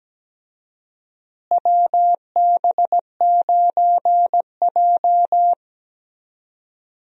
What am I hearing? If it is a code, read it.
Morse code, heard as a keyed tone, about 17 wpm, 700 Hz. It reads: WB9J
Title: USA Call Signs - 4 Characters - Repeated 3x - 17wpm